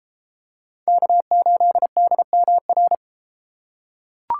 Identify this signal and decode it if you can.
Morse code, heard as a keyed tone, about 32 wpm, 700 Hz. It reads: K8DMR